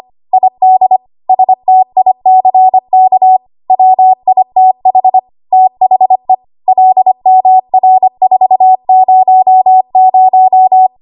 Morse code, heard as a keyed tone, about 25 wpm, 750 Hz. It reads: ID STICK WITH THE LMR400
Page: W1AW Web Code Practice Files